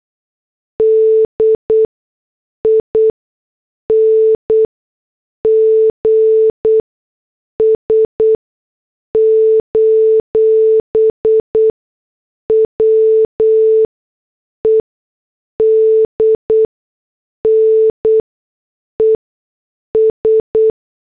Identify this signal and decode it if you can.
Morse code, heard as a keyed tone, about 8 wpm, spaced as for 6 wpm, 440 Hz. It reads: DINGS: WEDNES
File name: morse / headlines